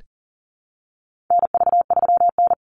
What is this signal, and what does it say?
Morse code, heard as a keyed tone, about 40 wpm, 700 Hz. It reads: NV3N
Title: Call Signs - 40wpm